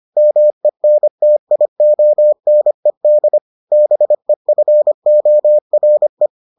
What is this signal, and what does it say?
Morse code, heard as a keyed tone, about 25 wpm, 600 Hz. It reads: MENTIONED BEFORE